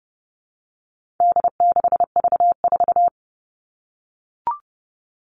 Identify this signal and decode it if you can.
Morse code, heard as a keyed tone, about 30 wpm, 700 Hz. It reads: D6V4
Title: All Cut Numbers - 4-Digit 30wpm